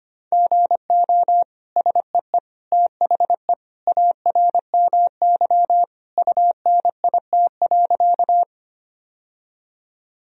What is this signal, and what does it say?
Morse code, heard as a keyed tone, about 25 wpm, 700 Hz. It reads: GO SEE THE ARMY UNIT.
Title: Sentences from Top 1000 Words - Review Part 2 - 25wpm